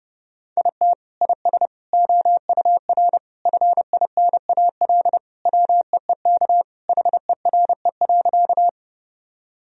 Morse code, heard as a keyed tone, about 32 wpm, 700 Hz. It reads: IT IS OUR FINAL WEEK HERE.
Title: Sentences from Top 1000 Words - Review Part 2 - 30wpm